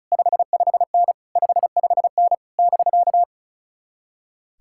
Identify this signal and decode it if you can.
Morse code, heard as a keyed tone, about 46 wpm, 700 Hz. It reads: 55N 55N <BK>